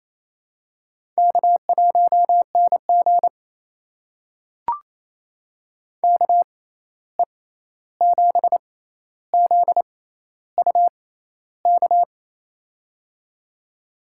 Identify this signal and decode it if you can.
Morse code, heard as a keyed tone, about 28 wpm, spaced as for 13 wpm, 700 Hz. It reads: K1NG KE7ZUK
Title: USA Call Signs - 6x Character Spacing - 28wpm